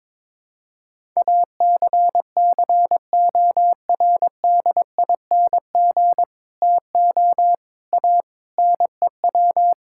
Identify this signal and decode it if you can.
Morse code, heard as a keyed tone, about 21 wpm, 700 Hz. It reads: ACCORDING TO A NEW
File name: practice / 22wpm / Sets of 7 Words - 22wpm